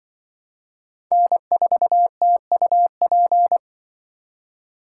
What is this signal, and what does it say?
Morse code, heard as a keyed tone, about 25 wpm, 700 Hz. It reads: N4TUP
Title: Speed Racing - USA 1x3 Call Signs 20wpm